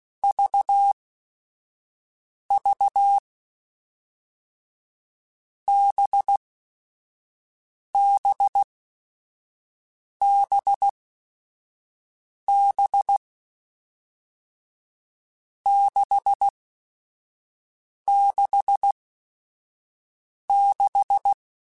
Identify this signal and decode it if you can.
Morse code, heard as a keyed tone, about 16 wpm, 800 Hz. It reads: V V B B B B 6 6 6
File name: Groups@5 / Groups@5 mp3s / VB46@5